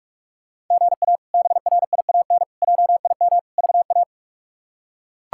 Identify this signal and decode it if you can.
Morse code, heard as a keyed tone, about 41 wpm, 700 Hz. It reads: GA BRIAN JIM VA
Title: K1USN SST QSO Practice - 45wpm